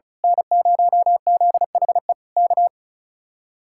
Morse code, heard as a keyed tone, about 32 wpm, 700 Hz. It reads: N0ZHE K
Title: Speed-Racing Common QSO Phrases - 35wpm